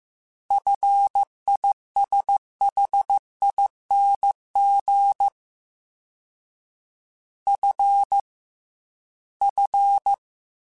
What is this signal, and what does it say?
Morse code, heard as a keyed tone, about 15 wpm, 800 Hz. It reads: FISHING F F
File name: F and L / F and L mp3 / F and L Prefixes 3x2 Twice @13